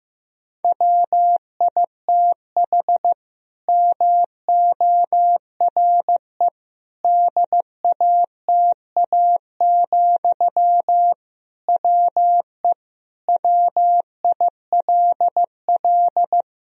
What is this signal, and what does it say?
Morse code, heard as a keyed tone, about 15 wpm, 700 Hz. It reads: WITH MORE DATA, WE WILL
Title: Sentences from Top 1000 Words - Review Part 3 - 15wpm